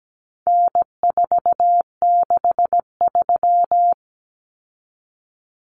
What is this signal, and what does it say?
Morse code, heard as a keyed tone, about 17 wpm, 700 Hz. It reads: N463
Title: Common Cut Numbers - 4-Digit 17wpm